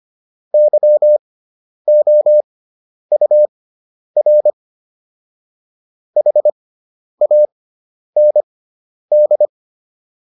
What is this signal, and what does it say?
Morse code, heard as a keyed tone, about 25 wpm, spaced as for 10 wpm, 600 Hz. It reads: YOUR HAND